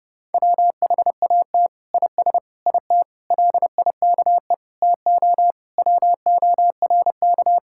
Morse code, heard as a keyed tone, about 30 wpm, 700 Hz. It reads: WHAT IS IT LIKE TO WORK
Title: Sentences from Top 1000 Words - Review Part 6 - 30wpm